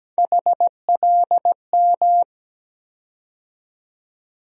Morse code, heard as a keyed tone, about 17 wpm, 700 Hz. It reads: HLM
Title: USA Call Sign Postfixes - 3 Letters 17wpm